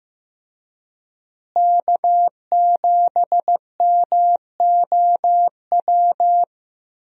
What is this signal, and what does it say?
Morse code, heard as a keyed tone, about 16 wpm, 700 Hz. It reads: K7MOW
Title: USA Call Signs - 5 Characters - Repeated 2x - 15wpm